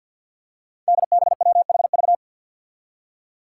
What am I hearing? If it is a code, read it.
Morse code, heard as a keyed tone, about 50 wpm, 700 Hz. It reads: DBWHV
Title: Five Challenging Letter-Number Combinations 50wpm